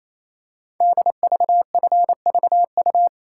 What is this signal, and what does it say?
Morse code, heard as a keyed tone, about 28 wpm, 700 Hz. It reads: DVFVU
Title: Code Groups - 5 Letters - 28wpm